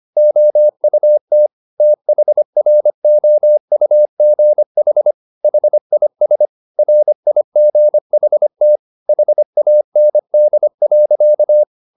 Morse code, heard as a keyed tone, about 25 wpm, 600 Hz. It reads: OUT THROUGH HIS RIGHT HAND.